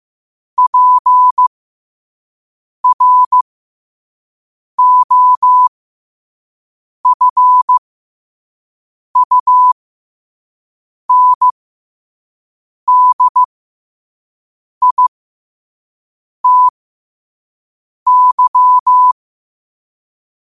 Morse code, heard as a keyed tone, about 15 wpm, spaced as for 5 wpm, 1000 Hz. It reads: PROFUNDITY
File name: qotdfiles / 5wpm